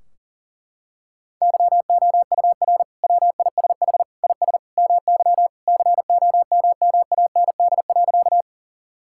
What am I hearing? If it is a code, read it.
Morse code, heard as a keyed tone, about 40 wpm, 700 Hz. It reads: YOUR WISH IS MY COMMAND.
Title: Sentences from Top 1600 Words 40wpm